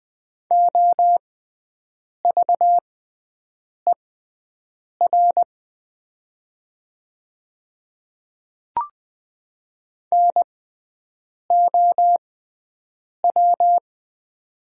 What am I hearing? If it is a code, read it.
Morse code, heard as a keyed tone, about 20 wpm, spaced as for 7 wpm, 700 Hz. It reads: OVER NOW
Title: 6x Character Spacing - Top 100 Words - 20wpm